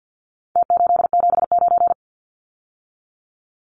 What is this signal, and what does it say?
Morse code, heard as a keyed tone, about 49 wpm, 700 Hz. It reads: T879
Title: Numbers - Contest Style - 4-Digit 50wpm